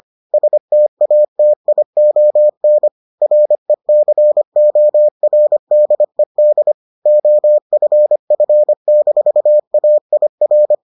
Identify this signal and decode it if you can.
Morse code, heard as a keyed tone, about 25 wpm, 600 Hz. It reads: STATION RECORDED OFF-AIR